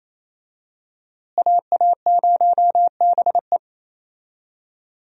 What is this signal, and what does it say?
Morse code, heard as a keyed tone, about 28 wpm, 700 Hz. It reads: AA0BE
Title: USA Call Signs - 5 Characters - Repeated 3x - 28wpm